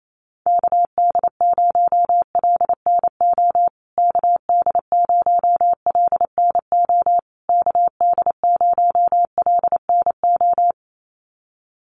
Morse code, heard as a keyed tone, about 28 wpm, 700 Hz. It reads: KD0LNO KD0LNO KD0LNO
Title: USA Call Signs Repeated 3x 28wpm